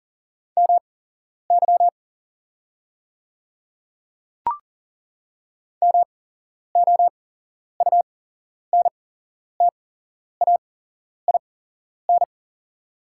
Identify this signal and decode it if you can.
Morse code, heard as a keyed tone, about 40 wpm, spaced as for 11 wpm, 690 Hz. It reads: MY MOUNTAIN
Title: 8x Character Spacing - Top 300 Words - 40wpm